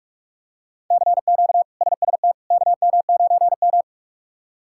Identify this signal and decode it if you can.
Morse code, heard as a keyed tone, about 46 wpm, 700 Hz. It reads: CQ SST KM9M
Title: K1USN SST QSO Practice - 45wpm